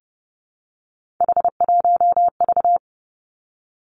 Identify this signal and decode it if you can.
Morse code, heard as a keyed tone, about 30 wpm, 700 Hz. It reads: H1V
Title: Three Challenging Letter-Number Combinations 30wpm